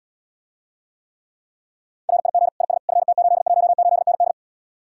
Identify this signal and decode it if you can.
Morse code, heard as a keyed tone, about 80 wpm, 700 Hz. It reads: RIG IS FT818ND